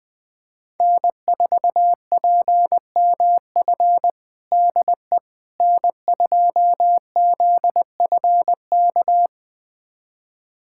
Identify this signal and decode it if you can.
Morse code, heard as a keyed tone, about 20 wpm, 700 Hz. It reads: N4PMF DE N2ZFK
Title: USA Callsign DE Callsign - 5 Characters - 20wpm